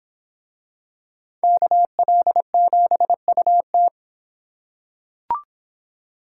Morse code, heard as a keyed tone, about 25 wpm, 700 Hz. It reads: KL7UT